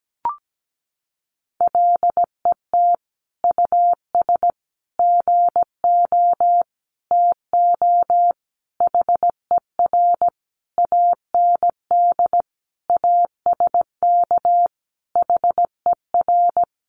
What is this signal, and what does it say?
Morse code, heard as a keyed tone, about 17 wpm, 700 Hz. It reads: LET US GO TO HER AND ASK HER
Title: Sentences from Top 1100 Words 17wpm